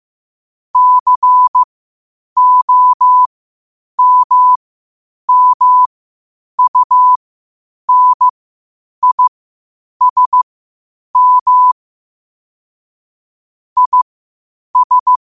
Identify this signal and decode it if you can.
Morse code, heard as a keyed tone, about 15 wpm, spaced as for 8 wpm, 1000 Hz. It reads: COMMUNISM IS
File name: 8wpm